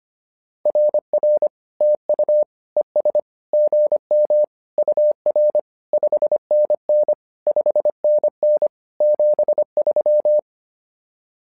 Morse code, heard as a keyed tone, about 25 wpm, 600 Hz. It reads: RR TU ES GM UR 5NN 5NN 73
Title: SOTA QSO Practice 25wpm